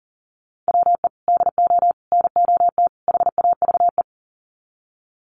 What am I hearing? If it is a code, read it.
Morse code, heard as a keyed tone, about 40 wpm, 700 Hz. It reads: WE DO NOT HAVE
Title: Sets of 4 Words 40wpm